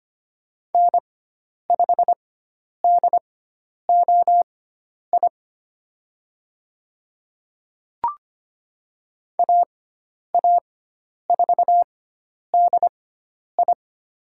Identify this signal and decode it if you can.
Morse code, heard as a keyed tone, about 25 wpm, spaced as for 10 wpm, 700 Hz. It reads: N5DOI AA4DI
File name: Lesson 16g - Letter c - Call Signs 25wpm